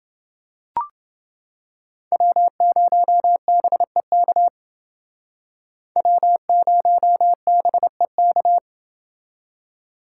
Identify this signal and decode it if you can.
Morse code, heard as a keyed tone, about 28 wpm, 700 Hz. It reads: W0BEK W0BEK